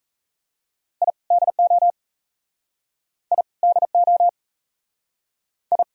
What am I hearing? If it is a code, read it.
Morse code, heard as a keyed tone, about 41 wpm, 700 Hz. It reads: I DO I DO I